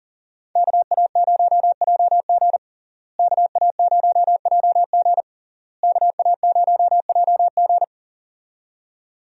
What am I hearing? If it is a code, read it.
Morse code, heard as a keyed tone, about 40 wpm, 700 Hz. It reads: KA0JG KA0JG KA0JG
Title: USA Call Signs - 5 Characters - Repeated 3x - 40wpm